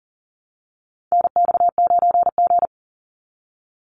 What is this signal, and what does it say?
Morse code, heard as a keyed tone, about 40 wpm, 700 Hz. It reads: NX9G
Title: Call Signs Repeated 3x - 40wpm